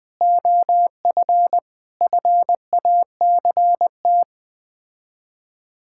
Morse code, heard as a keyed tone, about 20 wpm, 700 Hz. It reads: OF FACT
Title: Build A Phrase - 4-Word Phrases - 20wpm